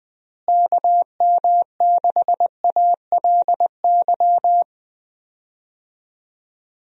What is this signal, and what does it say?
Morse code, heard as a keyed tone, about 21 wpm, 700 Hz. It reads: KM6ALY
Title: USA Call Signs - 6 Characters - Repeated 2x - 20wpm